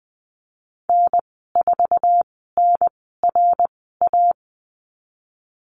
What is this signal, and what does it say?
Morse code, heard as a keyed tone, about 20 wpm, spaced as for 15 wpm, 700 Hz. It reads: N4NRA
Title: 055 Lesson 10g - Letter r - Call Signs 20wpm